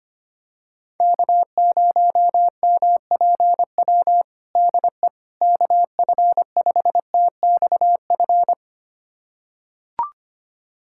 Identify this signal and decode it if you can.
Morse code, heard as a keyed tone, about 25 wpm, 700 Hz. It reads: K0MPW DE KF5TXF
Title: USA Callsign DE Callsign 25wpm